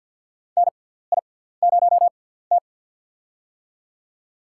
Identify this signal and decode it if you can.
Morse code, heard as a keyed tone, about 49 wpm, spaced as for 17 wpm, 700 Hz. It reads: NI0T
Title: USA Call Signs - 6x Character Spacing - 50wpm